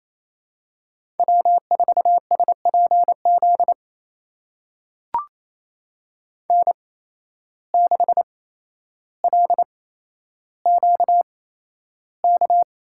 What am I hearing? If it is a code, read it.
Morse code, heard as a keyed tone, about 28 wpm, spaced as for 12 wpm, 700 Hz. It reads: W4SPZ N6LQK
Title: USA Call Signs - 5 Characters - 8x Character Spacing - 28wpm